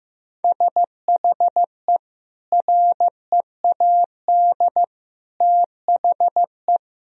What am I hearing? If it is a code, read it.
Morse code, heard as a keyed tone, about 15 wpm, 700 Hz. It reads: SHE READ THE
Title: Sentences from Top 1000 Words - Review Part 2 - 15wpm